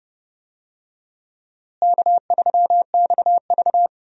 Morse code, heard as a keyed tone, about 29 wpm, 700 Hz. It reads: K3XV
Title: USA Call Signs - 4 Characters - Repeated 2x - 30wpm